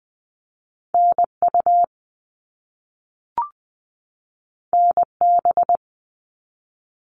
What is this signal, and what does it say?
Morse code, heard as a keyed tone, about 20 wpm, 700 Hz. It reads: NU NB
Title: Canadian Provinces and Territories 20wpm